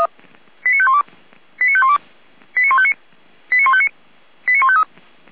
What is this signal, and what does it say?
Five-tone selective call (ZVEI), heard as quick runs of five tones, tones of about 70 ms. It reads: 89512 89512 89159 89159 89153